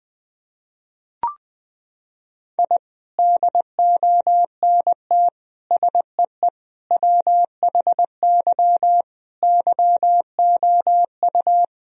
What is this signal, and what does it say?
Morse code, heard as a keyed tone, about 21 wpm, 700 Hz. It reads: I DONT SEE WHY YOU
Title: Sentences from Top 1000 Words - Review Part 2 - 20wpm